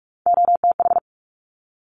Morse code, heard as a keyed tone, about 46 wpm, 700 Hz. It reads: OTH
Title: Top Three Letter Combinations in Words 45wpm